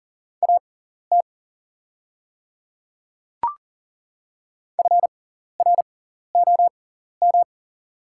Morse code, heard as a keyed tone, about 40 wpm, spaced as for 14 wpm, 700 Hz. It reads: AT FROM
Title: 6x Character Spacing - Top 100 Words - 40wpm